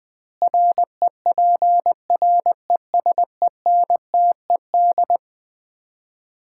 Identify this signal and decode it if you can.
Morse code, heard as a keyed tone, about 20 wpm, 700 Hz. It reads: REPRESENTED
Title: 2x Word-Spacing - Sets of 5 Words - 20wpm